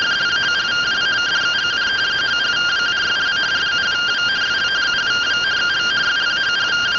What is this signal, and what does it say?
Two-tone teleprinter signal, about 170 Hz shift.